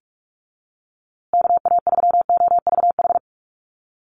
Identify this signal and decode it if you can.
Morse code, heard as a keyed tone, about 44 wpm, 700 Hz. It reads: KA3OVH
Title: USA Call Signs Repeated 2x 45wpm